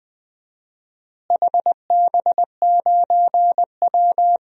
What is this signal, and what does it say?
Morse code, heard as a keyed tone, about 20 wpm, 700 Hz. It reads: HB9W
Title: DXCC-100 Call Signs - 4 Characters - Repeated 2x - 20wpm